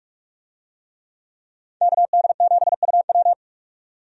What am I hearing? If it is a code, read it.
Morse code, heard as a keyed tone, about 46 wpm, 700 Hz. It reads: KD7UW
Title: USA Call Signs - 5 Characters - 45wpm